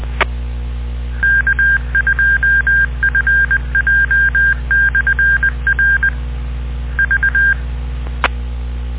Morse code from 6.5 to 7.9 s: V